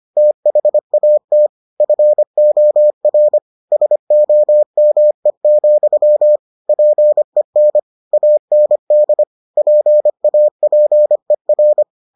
Morse code, heard as a keyed tone, about 25 wpm, 600 Hz. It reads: THAT FOR SOME, PEN AND PAPER